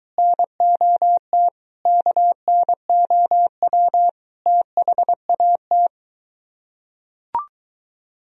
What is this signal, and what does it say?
Morse code, heard as a keyed tone, about 21 wpm, 700 Hz. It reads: NOT KNOW THAT